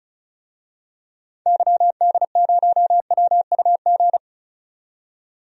Morse code, heard as a keyed tone, about 36 wpm, 700 Hz. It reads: YD0WUG
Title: DXCC-100 Call Signs - 6 Characters - 35wpm